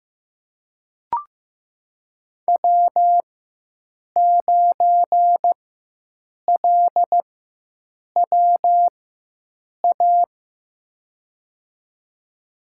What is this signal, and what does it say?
Morse code, heard as a keyed tone, about 15 wpm, spaced as for 7 wpm, 700 Hz. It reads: W9LWA
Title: USA Call Signs - 5 Characters - 4x Character Spacing - 15wpm